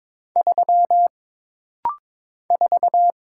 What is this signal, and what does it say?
Morse code, heard as a keyed tone, about 22 wpm, 700 Hz. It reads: 3 4
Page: This rapid-fire timing is geared to push your ICR (Instant Character Recognition) skills.